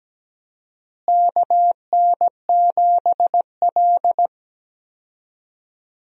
Morse code, heard as a keyed tone, about 17 wpm, 700 Hz. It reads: KN7L
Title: USA Call Signs - 4 Characters - Repeated 3x - 17wpm